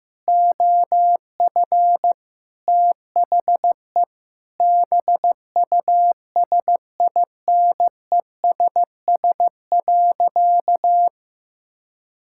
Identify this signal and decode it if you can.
Morse code, heard as a keyed tone, about 15 wpm, 700 Hz. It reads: OF THE BUSINESS.
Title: Sentences from Top 1000 Words - Review Part 6 - 15wpm